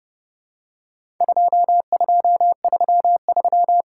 Morse code, heard as a keyed tone, about 32 wpm, 700 Hz. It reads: 2233
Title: Worked All Europe DX Contest - QTC Exchange - 30wpm